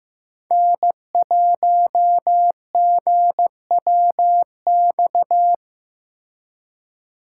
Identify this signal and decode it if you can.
Morse code, heard as a keyed tone, about 18 wpm, 700 Hz. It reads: N1GWX
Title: Speed Racing - USA 1x3 Call Signs 15wpm